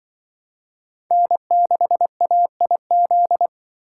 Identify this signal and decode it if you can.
Morse code, heard as a keyed tone, about 24 wpm, 700 Hz. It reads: N6AIZ
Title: Speed Racing - USA 1x3 Call Signs 20wpm